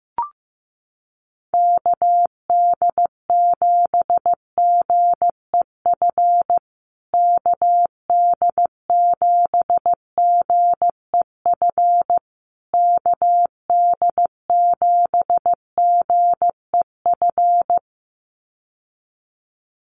Morse code, heard as a keyed tone, about 15 wpm, 700 Hz. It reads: KD7GEF KD7GEF KD7GEF
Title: USA Call Signs Repeated 3x 15wpm